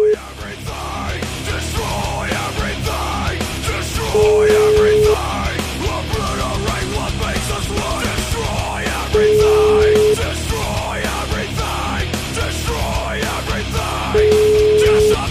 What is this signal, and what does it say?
Ringback tone (Europe, 425 Hz), with noise.